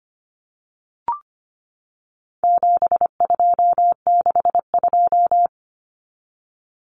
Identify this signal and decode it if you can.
Morse code, heard as a keyed tone, about 25 wpm, 700 Hz. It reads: 7262
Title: Common Cut Numbers - 4-Digit 25wpm